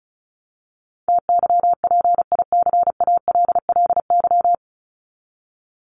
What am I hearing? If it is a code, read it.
Morse code, heard as a keyed tone, about 34 wpm, 700 Hz. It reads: TYPICALLY
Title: LY Suffix - Top 100 Words with Base Words 35wpm